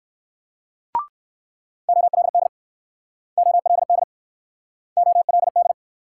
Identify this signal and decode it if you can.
Morse code, heard as a keyed tone, about 60 wpm, 700 Hz. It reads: OLD OLD OLD